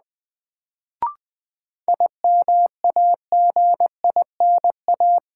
Morse code, heard as a keyed tone, about 20 wpm, 700 Hz. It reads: IMAGINA
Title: Sentences from Top 1300 Words - New Word Drill Alt - 20wpm